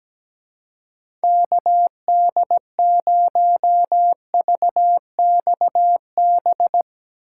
Morse code, heard as a keyed tone, about 17 wpm, 700 Hz. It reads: KD0VXB
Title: USA Call Signs - 6 Characters - 2x Character Spacing - 17wpm